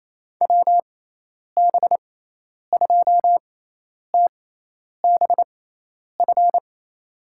Morse code, heard as a keyed tone, about 28 wpm, spaced as for 10 wpm, 700 Hz. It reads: WB2TBF
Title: USA Call Signs - 6x Character Spacing - 28wpm